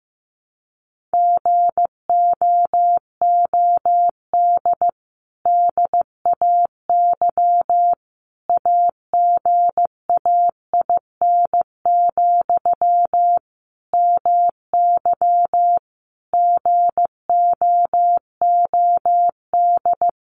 Morse code, heard as a keyed tone, about 15 wpm, 700 Hz. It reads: GOOD DAY AGAIN, MY GOOD
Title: Sentences from Top 1100 Words 15wpm